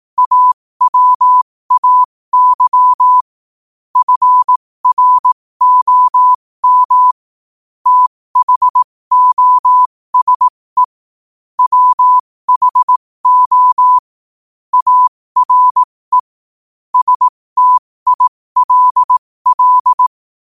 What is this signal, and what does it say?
Morse code, heard as a keyed tone, about 18 wpm, spaced as for 15 wpm, 1000 Hz. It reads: AWAY FROM THOSE WHO ARE STILL
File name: qotdfiles / 15wpm